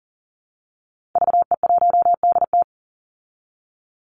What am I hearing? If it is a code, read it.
Morse code, heard as a keyed tone, about 40 wpm, 700 Hz. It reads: VE1DT
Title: Call Signs Repeated 3x - 40wpm